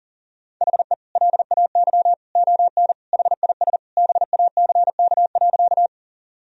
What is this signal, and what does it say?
Morse code, heard as a keyed tone, about 40 wpm, 700 Hz. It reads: HE LAY ON HIS BACK.
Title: Sentences from Top 1000 Words - Review Part 2 - 40wpm